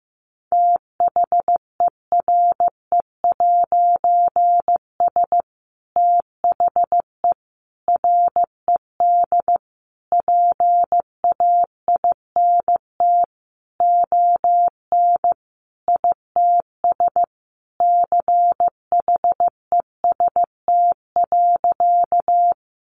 Morse code, heard as a keyed tone, about 15 wpm, 700 Hz. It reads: THERE'S THE RED PAINT ON ITS CHEST.
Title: A Sound Of Thunder 15wpm